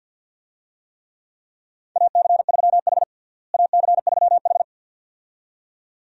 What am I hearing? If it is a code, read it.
Morse code, heard as a keyed tone, about 52 wpm, 700 Hz. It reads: AC3H AC3H
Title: USA Call Signs - 4 Characters - Repeated 2x - 50wpm